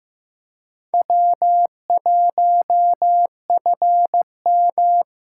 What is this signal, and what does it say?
Morse code, heard as a keyed tone, about 15 wpm, 700 Hz. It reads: W1FM
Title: USA Call Signs - 4 Characters - 8x Character Spacing - 15wpm